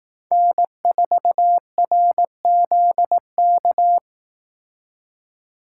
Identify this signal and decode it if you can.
Morse code, heard as a keyed tone, about 18 wpm, 700 Hz. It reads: N4RZK
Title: Speed Racing - USA 1x3 Call Signs 15wpm